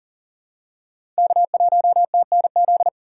Morse code, heard as a keyed tone, about 40 wpm, 700 Hz. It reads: K1TNZ
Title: USA Call Signs - 5 Characters - 2x Character Spacing - 40wpm